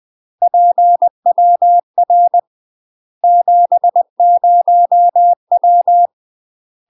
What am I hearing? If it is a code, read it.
Morse code, heard as a keyed tone, about 20 wpm, 700 Hz. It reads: PWR 70W
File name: UK MASTER QSOs TRAINING / 20WPM / UK MASTER QSOs 0023